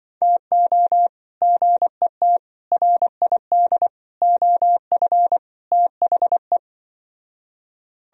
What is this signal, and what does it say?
Morse code, heard as a keyed tone, about 24 wpm, 700 Hz. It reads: TO GET RID OF THE
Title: Speed-Racing - Sets of 5 Words 20wpm